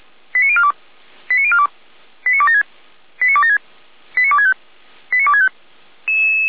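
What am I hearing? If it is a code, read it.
Five-tone selective call (ZVEI), heard as quick runs of five tones, tones of about 70 ms. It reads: 89042 89042 89276 89276 89265 89265
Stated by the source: ZVEI five-tone call received on SDR.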